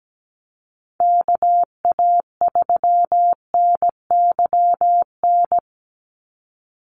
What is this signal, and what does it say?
Morse code, heard as a keyed tone, about 17 wpm, 700 Hz. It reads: KA3NYN
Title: USA Call Signs 17wpm